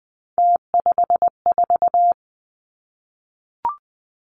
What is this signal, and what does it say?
Morse code, heard as a keyed tone, about 20 wpm, 700 Hz. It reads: T54
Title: Three Challenging Letter-Number Combinations 20wpm